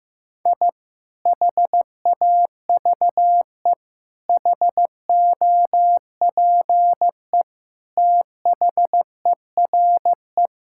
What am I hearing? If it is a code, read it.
Morse code, heard as a keyed tone, about 15 wpm, 700 Hz. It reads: I HAVE HOPE THERE
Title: Sentences from Top 1000 Words - Review Part 5 - 15wpm